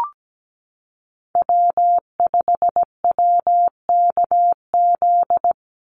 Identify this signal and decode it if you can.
Morse code, heard as a keyed tone, about 17 wpm, 700 Hz. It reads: W5WKZ
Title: USA Call Signs 17wpm